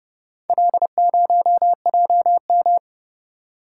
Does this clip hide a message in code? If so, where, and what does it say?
Yes: Morse code from 0.5 to 2.8 s, heard as a keyed tone, about 30 wpm, 700 Hz. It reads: L0JM